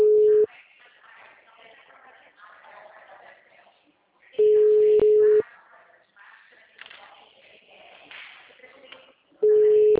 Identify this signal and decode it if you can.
Ringback tone (Europe, 425 Hz).
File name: out- -2025